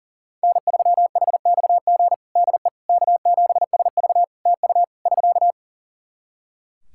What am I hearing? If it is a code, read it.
Morse code, heard as a keyed tone, about 40 wpm, 700 Hz. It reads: N3HXG DE K7SV TU <SK>
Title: K1USN SST QSO Practice 40wpm